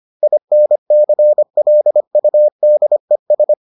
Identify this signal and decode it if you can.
Morse code, heard as a keyed tone, about 25 wpm, 600 Hz. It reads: INCLUDES